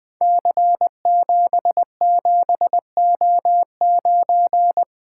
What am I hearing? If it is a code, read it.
Morse code, heard as a keyed tone, about 20 wpm, 700 Hz. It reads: C77O9
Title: Code Groups - 5 Letter:Numbers - 20wpm